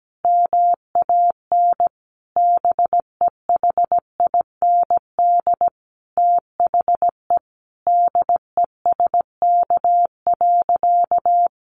Morse code, heard as a keyed tone, about 17 wpm, 700 Hz. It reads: MAN BEHIND THE DESK.
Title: A Sound Of Thunder 17wpm